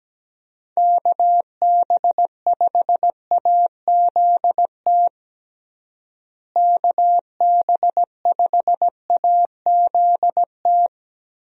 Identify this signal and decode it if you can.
Morse code, heard as a keyed tone, about 17 wpm, 700 Hz. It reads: KB5AZT KB5AZT
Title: USA Call Signs - 6 Characters - Repeated 3x - 17wpm